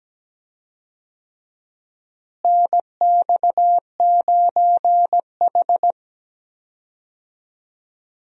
Morse code, heard as a keyed tone, about 17 wpm, 700 Hz. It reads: NX9H
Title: USA Call Signs - 4 Characters - Repeated 3x - 17wpm